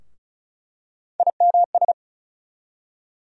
Morse code, heard as a keyed tone, about 36 wpm, 700 Hz. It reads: IMS
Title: USA Call Sign Postfixes - 3 Letters 35wpm